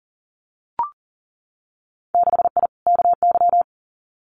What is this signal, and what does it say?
Morse code, heard as a keyed tone, about 37 wpm, 700 Hz. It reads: 6I KY
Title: Winter Field Day Exchanges Encore 40wpm